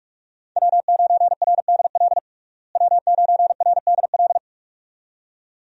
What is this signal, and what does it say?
Morse code, heard as a keyed tone, about 47 wpm, 700 Hz. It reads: W9RDL W9RDL
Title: USA Call Signs - 5 Characters - Repeated 2x - 45wpm